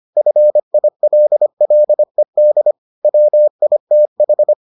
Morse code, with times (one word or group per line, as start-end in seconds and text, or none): 0.2-2.7 FILLED
3.0-4.5 WITH